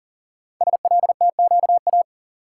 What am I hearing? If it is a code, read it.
Morse code, heard as a keyed tone, about 43 wpm, 690 Hz. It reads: SLTQA
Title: Code Groups - 5 Letters - 40wpm